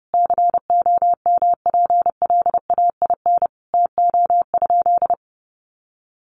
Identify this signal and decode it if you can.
Morse code, heard as a keyed tone, about 30 wpm, 700 Hz. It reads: COMPLAIN TO?
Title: Sentences from Top 1300 Words 30wpm